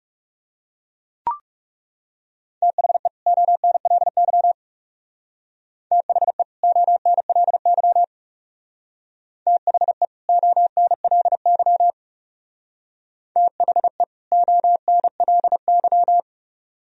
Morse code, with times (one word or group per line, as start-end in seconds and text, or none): 2.6-3.1 THE
3.3-4.5 ONLY
5.9-6.4 THE
6.6-8.0 ONLY
9.5-10.1 THE
10.3-11.9 ONLY
13.4-14.0 THE
14.3-16.2 ONLY